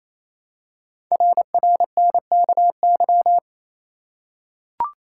Morse code, heard as a keyed tone, about 27 wpm, 700 Hz. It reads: RRNKY
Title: Code Groups - 5 Letters - 28wpm